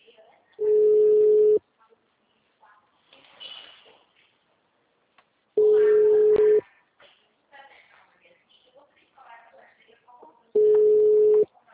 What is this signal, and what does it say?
Ringback tone (Europe, 425 Hz), with speech.
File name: out- -2029- - - .65277